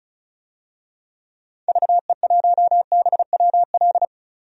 Morse code, heard as a keyed tone, about 36 wpm, 700 Hz. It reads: VE1BWL
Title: DXCC-100 Call Signs - 6 Characters - 35wpm